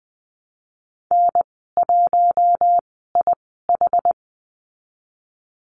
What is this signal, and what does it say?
Morse code, heard as a keyed tone, about 20 wpm, spaced as for 14 wpm, 700 Hz. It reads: N1IH
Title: 140 Lesson 23g - Letter f - Call Signs 20wpm